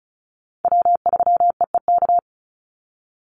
Morse code, heard as a keyed tone, about 34 wpm, 700 Hz. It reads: W3EEK
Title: Call Signs - 35wpm